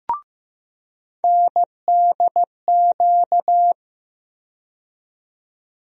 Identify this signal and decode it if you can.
Morse code, heard as a keyed tone, about 15 wpm, 700 Hz. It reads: NDQ
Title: USA Call Sign Postfixes - 3 Letters 15wpm